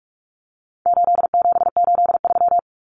Morse code, heard as a keyed tone, about 44 wpm, 700 Hz. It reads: 8783
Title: Common Cut Numbers - 4-Digit 45wpm